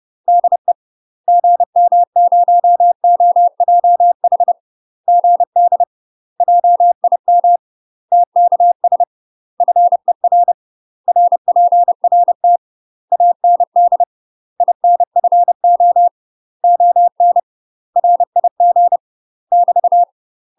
Morse code, heard as a keyed tone, about 30 wpm, 700 Hz. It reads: DE GM0OJH GD JIM TKS FER RPRT AND INFO ON RIG <BT>
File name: UK MASTER QSOs TRAINING/30WPM